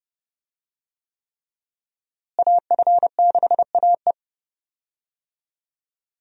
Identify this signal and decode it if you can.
Morse code, heard as a keyed tone, about 30 wpm, 700 Hz. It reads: AF6AE
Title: USA Call Signs - 5 Characters - Repeated 3x - 30wpm